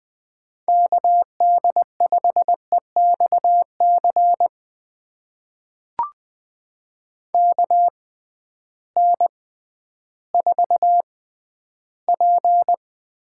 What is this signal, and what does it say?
Morse code, heard as a keyed tone, about 20 wpm, spaced as for 12 wpm, 700 Hz. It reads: KD5EXC KN4P
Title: USA Call Signs - 6 Characters - 6x Character Spacing - 20wpm